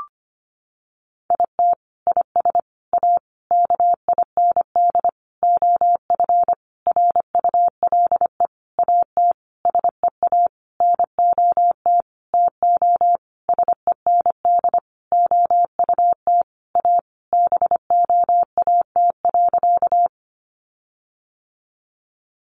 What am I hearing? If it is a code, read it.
Morse code, heard as a keyed tone, about 25 wpm, 700 Hz. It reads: IT IS A KIND OF RULE AT SEA NOT TO SEND OUT A BOAT.
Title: Sentences from Top 1500 Words 25wpm